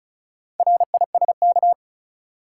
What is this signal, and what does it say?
Morse code, heard as a keyed tone, about 36 wpm, 700 Hz. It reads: RISK
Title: Sentences from Top 600 Words - New Word Drill - 35wpm